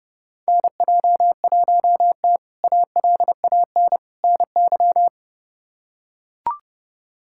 Morse code, heard as a keyed tone, about 30 wpm, 700 Hz. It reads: NJ1T ALAN NY
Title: K1USN SST QSO Practice - 30wpm